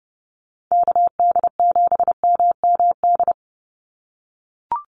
Morse code, heard as a keyed tone, about 30 wpm, 700 Hz. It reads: KD7MMD
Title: USA Call Signs 30wpm